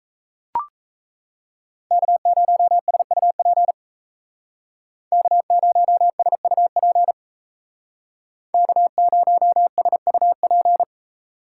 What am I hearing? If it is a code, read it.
Morse code, heard as a keyed tone, about 35 wpm, 700 Hz. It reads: K0SUP K0SUP K0SUP